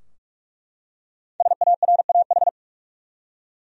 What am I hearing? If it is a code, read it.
Morse code, heard as a keyed tone, about 43 wpm, 700 Hz. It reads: SARAH
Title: USA Common Names - Max 5 Letters - 45wpm